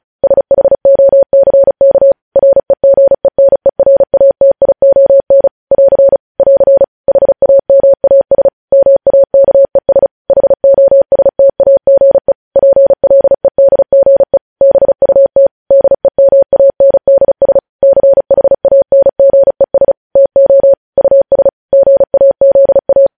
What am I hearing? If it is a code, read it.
Morse code, heard as a keyed tone, about 36 wpm, 550 Hz. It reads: SHOCK REGENERATION <AR> <AR> HAMAS MAKES HOSTAGE PLEDGE BUT DEMANDS CHANGES TO US GAZA
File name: podcast / News headlines-1x-35x35